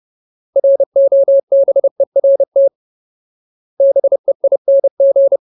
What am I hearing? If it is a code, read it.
Morse code, heard as a keyed tone, about 30 wpm, 550 Hz. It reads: ROBERT BEING
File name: Ultimate Word List / 30WPM Ultimate